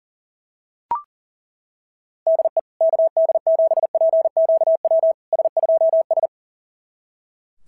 Morse code, heard as a keyed tone, about 40 wpm, 690 Hz. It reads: DE KD7PQW S2S
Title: SOTA QSO Practice 40wpm